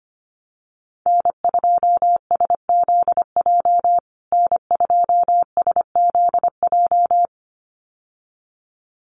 Morse code, heard as a keyed tone, about 25 wpm, 700 Hz. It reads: N2SZJ N2SZJ
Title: USA Call Signs Repeated 2x 25wpm